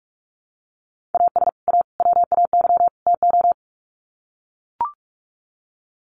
Morse code, heard as a keyed tone, about 44 wpm, 700 Hz. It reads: AS A WAY TO
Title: Sets of 4 Words 45wpm